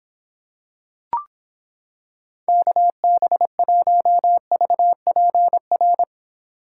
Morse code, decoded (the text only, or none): KB1VPR